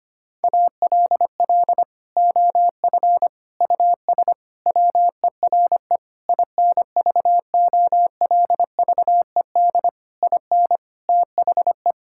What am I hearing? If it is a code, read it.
Morse code, heard as a keyed tone, about 25 wpm, 700 Hz. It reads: ALL OF US WERE INVOLVED IN THE
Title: Sentences from Top 1000 Words - Review Part 5 - 25wpm